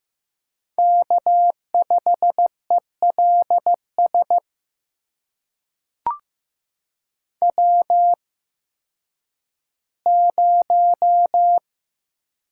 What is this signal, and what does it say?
Morse code, heard as a keyed tone, about 15 wpm, spaced as for 10 wpm, 700 Hz. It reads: K5ELS W0
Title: USA Call Signs - 5 Characters - 8x Character Spacing - 15wpm